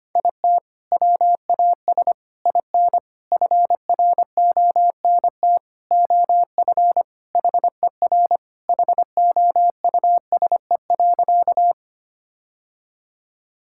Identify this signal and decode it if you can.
Morse code, heard as a keyed tone, about 25 wpm, 700 Hz. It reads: IT WAS IN FRONT OF HER HOUSE.
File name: practice / 25wpm / Sentences with 4-Word Phrases - 25wpm